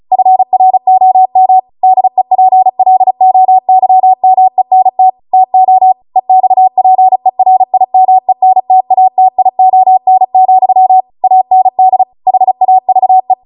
Morse code, heard as a keyed tone, about 36 wpm, 750 Hz. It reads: FROM DEPLOYMENT TO EXPERIMENTATION, AND HAVE
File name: Morse / 35WPM